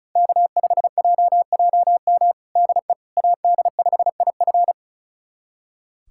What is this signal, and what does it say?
Morse code, heard as a keyed tone, about 36 wpm, 700 Hz. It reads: K5JJM DE AD5IF
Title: USA Callsign DE Callsign - 5 Characters - 35wpm